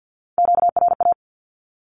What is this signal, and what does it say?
Morse code, heard as a keyed tone, about 50 wpm, 700 Hz. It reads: QRA
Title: Q-Codes 50wpm